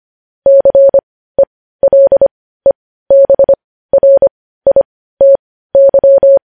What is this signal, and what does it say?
Morse code, heard as a keyed tone, about 25 wpm, spaced as for 15 wpm, 550 Hz. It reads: CELEBRITY
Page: news-headlines-in-morse-code-at-15-wpm